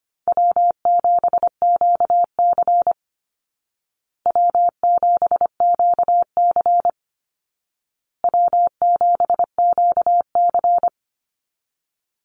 Morse code, heard as a keyed tone, about 25 wpm, 700 Hz. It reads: W7QC W7QC W7QC
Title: Call Signs Repeated 2x - 25wpm